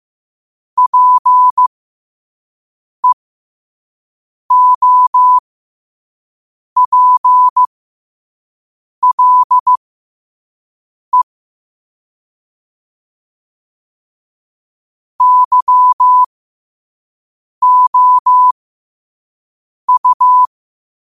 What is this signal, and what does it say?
Morse code, heard as a keyed tone, about 15 wpm, spaced as for 5 wpm, 1000 Hz. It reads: PEOPLE YOU
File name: qotdfiles / 5wpm